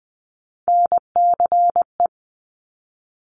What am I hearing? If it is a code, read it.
Morse code, heard as a keyed tone, about 20 wpm, 700 Hz. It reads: NCE
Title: Top Three Letter Combinations in Words 20wpm